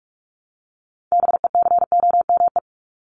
Morse code, heard as a keyed tone, about 44 wpm, 700 Hz. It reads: BECOME
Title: Top 501-750 Words 45wpm